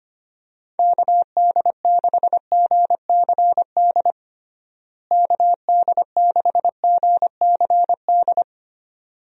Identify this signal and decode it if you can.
Morse code, heard as a keyed tone, about 25 wpm, 700 Hz. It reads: KD6GCD KD6GCD
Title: USA Call Signs - 6 Characters - Repeated 2x - 25wpm